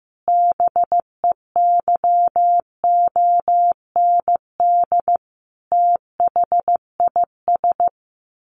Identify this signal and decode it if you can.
Morse code, heard as a keyed tone, about 15 wpm, 700 Hz. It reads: BEYOND THIS
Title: A Sound Of Thunder 15wpm